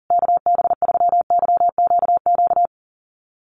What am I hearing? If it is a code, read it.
Morse code, heard as a keyed tone, about 40 wpm, 700 Hz. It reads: KB3YQQ
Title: USA Call Signs Repeated 3x 40wpm